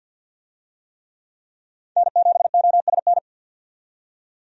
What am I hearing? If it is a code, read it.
Morse code, heard as a keyed tone, about 50 wpm, 700 Hz. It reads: N7OSN
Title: USA Call Signs - 5 Characters - 50wpm